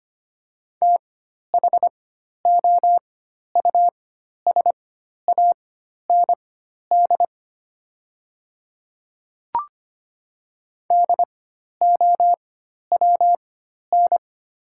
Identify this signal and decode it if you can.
Morse code, heard as a keyed tone, about 25 wpm, spaced as for 12 wpm, 700 Hz. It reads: THOUSAND DOWN
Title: 4x Character Spacing - Top 500 Words - 25wpm